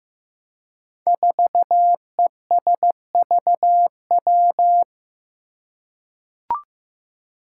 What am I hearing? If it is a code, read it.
Morse code, heard as a keyed tone, about 15 wpm, 700 Hz. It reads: 4ESVW
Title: Five Challenging Letter-Number Combinations 15wpm